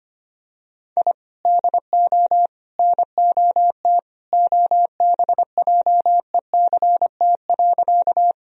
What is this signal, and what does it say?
Morse code, heard as a keyed tone, about 25 wpm, 700 Hz. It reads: I DO NOT OBJECT.
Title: Sentences from Top 1000 Words - Review Part 3 - 25wpm